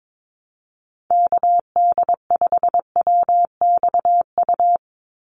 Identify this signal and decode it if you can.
Morse code, heard as a keyed tone, about 22 wpm, 700 Hz. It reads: KD5WXU
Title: USA Call Signs 22wpm